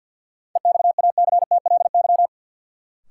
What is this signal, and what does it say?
Morse code, heard as a keyed tone, about 46 wpm, 700 Hz. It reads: EXACTLY
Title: Sentences from Top 900 Words - New Word Drill - 50wpm